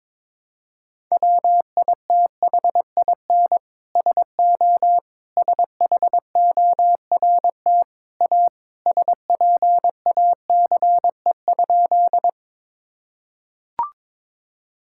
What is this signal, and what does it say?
Morse code, heard as a keyed tone, about 22 wpm, 700 Hz. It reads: WITHIN SO SHORT A SPACE?
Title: Sentences from Top 1000 Words - Review Part 5 - 22wpm